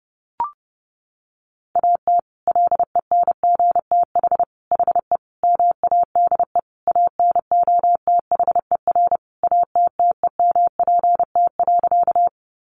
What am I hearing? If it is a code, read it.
Morse code, heard as a keyed tone, about 30 wpm, 700 Hz. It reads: AT LENGTH HE MADE ANOTHER ATTEMPT.
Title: Sentences from Top 1000 Words 30wpm